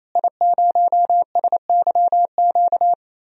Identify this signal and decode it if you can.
Morse code, heard as a keyed tone, about 28 wpm, 700 Hz. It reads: I0SYQ
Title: DXCC-100 Call Signs - 5 Characters - Repeated 3x - 28wpm